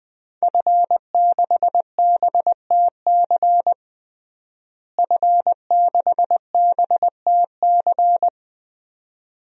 Morse code, heard as a keyed tone, about 20 wpm, 700 Hz. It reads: F6BTC F6BTC
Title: DXCC-100 Call Signs - 5 Characters - Repeated 2x - 20wpm